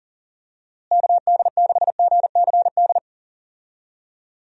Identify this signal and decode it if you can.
Morse code, heard as a keyed tone, about 40 wpm, 700 Hz. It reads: KD6GCD
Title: USA Call Signs - 6 Characters - Repeated 2x - 40wpm